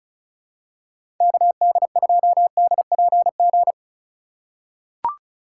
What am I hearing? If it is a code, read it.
Morse code, heard as a keyed tone, about 34 wpm, 700 Hz. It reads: KD2DPG
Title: USA Call Signs - 6 Characters - 2x Character Spacing - 35wpm